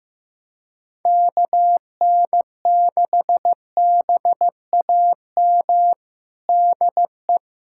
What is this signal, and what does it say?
Morse code, heard as a keyed tone, about 15 wpm, 700 Hz. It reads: KN6BAM DE
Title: USA Callsign DE Callsign - 6 Characters - 15wpm